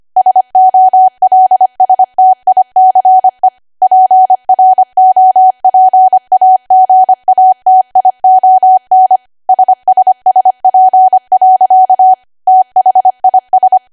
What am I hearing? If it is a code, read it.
Morse code, heard as a keyed tone, about 25 wpm, 750 Hz. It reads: SOLSTICE PROPAGATION SSSP. THIS